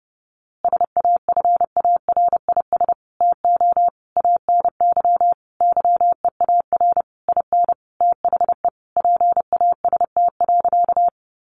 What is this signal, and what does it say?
Morse code, heard as a keyed tone, about 30 wpm, 700 Hz. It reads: SAFARIS TO ANY YEAR IN THE PAST.
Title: A Sound Of Thunder 30wpm